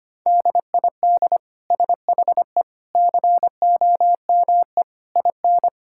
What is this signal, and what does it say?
Morse code, heard as a keyed tone, about 25 wpm, 700 Hz. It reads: DID SHE COME IN
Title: Sentences from Top 1000 Words - Review Part 5 - 25wpm